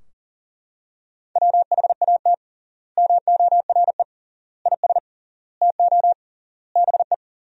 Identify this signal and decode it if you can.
Morse code, heard as a keyed tone, about 40 wpm, 700 Hz. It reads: WHAT MORE IS TO BE
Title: 3x Word-Spacing - Sentences from Top 500 Words - 40wpm